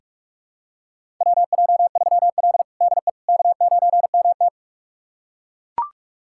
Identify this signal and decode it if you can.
Morse code, heard as a keyed tone, about 41 wpm, 700 Hz. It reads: WJ3L DE K9MT
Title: USA Callsign DE Callsign - 4 Characters - 45wpm